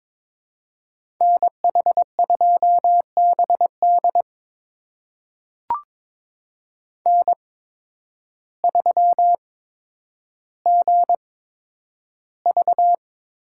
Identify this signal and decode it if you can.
Morse code, heard as a keyed tone, about 22 wpm, spaced as for 10 wpm, 700 Hz. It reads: NH2BD N3GV
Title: USA Call Signs - 5 Characters - 8x Character Spacing - 22wpm